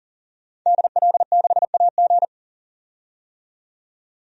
Morse code, heard as a keyed tone, about 40 wpm, 700 Hz. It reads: DL6AG
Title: DXCC-100 Call Signs - 5 Characters - 40wpm